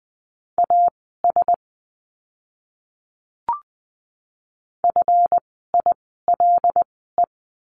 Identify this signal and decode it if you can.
Morse code, heard as a keyed tone, about 20 wpm, spaced as for 14 wpm, 700 Hz. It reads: AS FILE
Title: 139 Lesson 23f - Letter f - Words 20wpm